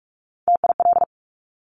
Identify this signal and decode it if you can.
Morse code, heard as a keyed tone, about 46 wpm, 690 Hz. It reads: TIL
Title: Top Three Letter Combinations in Words 45wpm